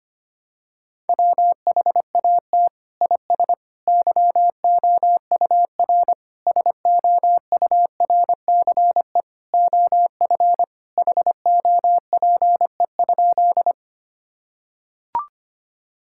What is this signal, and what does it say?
Morse code, heard as a keyed tone, about 25 wpm, 700 Hz. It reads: WHAT IS YOUR SOURCE OF HOPE?
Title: Sentences from Top 1000 Words - Review Part 5 - 25wpm